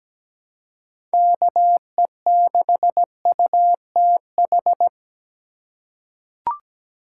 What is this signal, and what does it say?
Morse code, heard as a keyed tone, about 17 wpm, 700 Hz. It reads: KE6UTH
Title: USA Call Signs - 6 Characters - 4x Character Spacing - 17wpm